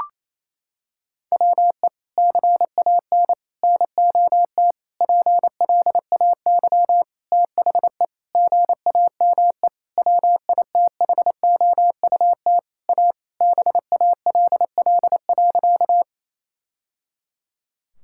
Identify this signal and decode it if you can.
Morse code, heard as a keyed tone, about 28 wpm, 700 Hz. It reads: WE CAN NOT PLAY THE GAME WITHOUT A BALL.
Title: Sentences from Top 1000 Words - Review Part 6 - 28wpm